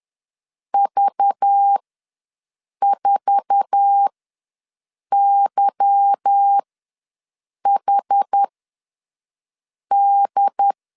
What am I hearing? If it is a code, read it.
Morse code, heard as a keyed tone, about 10 wpm, spaced as for 6 wpm, 800 Hz. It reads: V4YHD
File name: LEZIONI / 8 YJ45 / 8 YJ45 ws 10wpm 3843